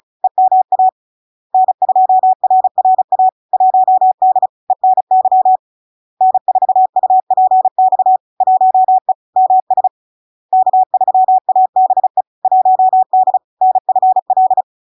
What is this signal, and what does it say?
Morse code, heard as a keyed tone, about 35 wpm, 750 Hz. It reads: EMA N2RRA 1D ENY N4UPX 1E MS K3ABE 1D NFL
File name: LARK-FD-2022-Log-in-CW-35wpm-750-Hz